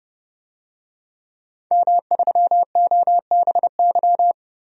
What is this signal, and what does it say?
Morse code, heard as a keyed tone, about 32 wpm, 700 Hz. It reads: M3OBY
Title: DXCC-100 Call Signs - 5 Characters - 30wpm